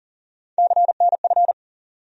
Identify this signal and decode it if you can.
Morse code, heard as a keyed tone, about 38 wpm, 700 Hz. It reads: CNF
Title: USA Call Sign Postfixes - 3 Letters 40wpm